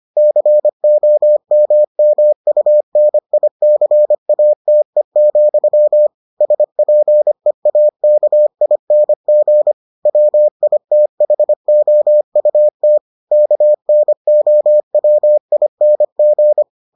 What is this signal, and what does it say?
Morse code, heard as a keyed tone, about 25 wpm, 600 Hz. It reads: COMMUNICATE, SPEAKING WITHOUT KNOWING